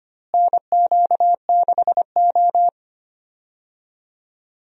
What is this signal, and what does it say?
Morse code, heard as a keyed tone, about 25 wpm, 700 Hz. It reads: NQ6O
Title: USA Call Signs - 4 Characters - Repeated 3x - 25wpm